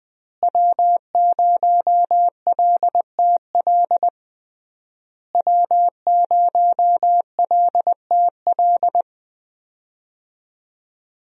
Morse code, heard as a keyed tone, about 21 wpm, 700 Hz. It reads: W0LTL W0LTL
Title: USA Call Signs - 5 Characters - Repeated 3x - 20wpm